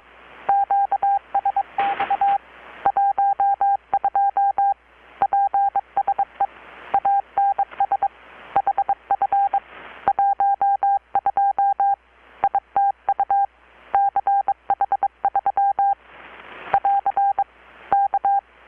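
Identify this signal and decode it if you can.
Morse code, heard as a keyed tone, about 23 wpm, 800 Hz. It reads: QSX 12 PSE ANS HF 12 ITU CH3 <AR> K
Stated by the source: Morse from KPH's Night of Nights XX maritime radiotelegraphy event, received in Chiapas, Mexico about 3,800 km away on a Kenwood R-600 with a grounded 10 m long-wire antenna.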